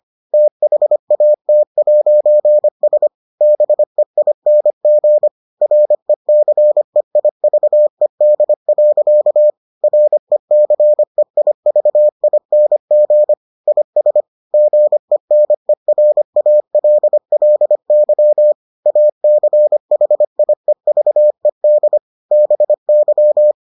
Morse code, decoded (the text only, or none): THAT'S BEING RECEIVED. RECEIVING IS GENERALLY ACHIEVED BY